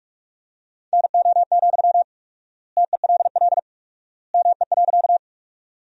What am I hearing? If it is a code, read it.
Morse code, heard as a keyed tone, about 46 wpm, 700 Hz. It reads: NO, TELL ME.
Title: 4x Word-Spacing - Sentences from Top 200 Words - 45wpm